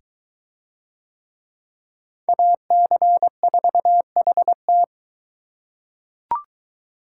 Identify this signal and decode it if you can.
Morse code, heard as a keyed tone, about 21 wpm, 700 Hz. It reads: AC4HT